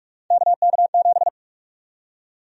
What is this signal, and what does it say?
Morse code, heard as a keyed tone, about 46 wpm, 700 Hz. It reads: KK7
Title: USA Call Sign Prefixes - 45wpm